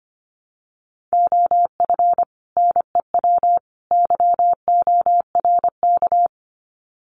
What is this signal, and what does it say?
Morse code, heard as a keyed tone, about 25 wpm, 700 Hz. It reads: OF NEW YORK
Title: Sets of 3 Words 25wpm